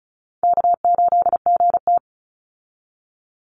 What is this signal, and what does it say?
Morse code, heard as a keyed tone, about 34 wpm, 700 Hz. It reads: K8GT
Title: Easy Call Signs Repeated 3x - 35wpm